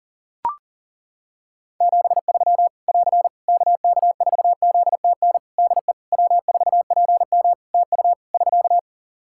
Morse code, heard as a keyed tone, about 40 wpm, 700 Hz. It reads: 73 <AR> KK4ZTN DE W4PM TU <SK>
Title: Practice QSO - Exchange 4 - Ending - 40wpm